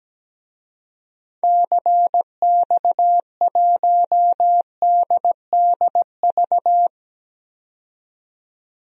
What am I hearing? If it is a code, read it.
Morse code, heard as a keyed tone, about 17 wpm, 700 Hz. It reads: CX1DDV
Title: DXCC-100 Call Signs - 6 Characters - 17wpm